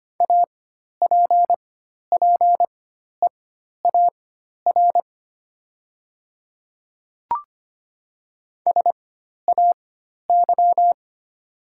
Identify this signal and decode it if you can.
Morse code, heard as a keyed tone, about 25 wpm, spaced as for 12 wpm, 700 Hz. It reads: APPEAR SAY
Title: 4x Character Spacing - Top 500 Words - 25wpm